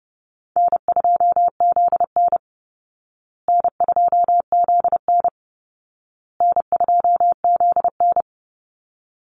Morse code, heard as a keyed tone, about 30 wpm, 700 Hz. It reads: N2ZN N2ZN N2ZN
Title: Call Signs Repeated 2x - 30wpm